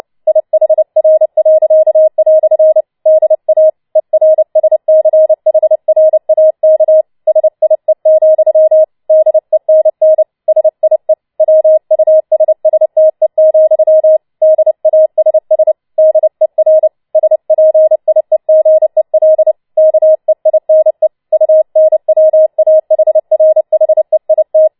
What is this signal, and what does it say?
Morse code, heard as a keyed tone, about 29 wpm, 630 Hz. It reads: IHR." DA ERSCHRAK SIE, DENN SIE WUSSTE, DASS DER SPIEGEL KEINE UNWAHRHEIT
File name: Schneewittchen Teil3 30wpm